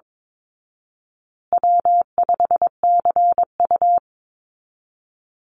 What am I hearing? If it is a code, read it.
Morse code, heard as a keyed tone, about 22 wpm, 700 Hz. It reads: W5CU
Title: Easy Call Signs Repeated 3x - 22wpm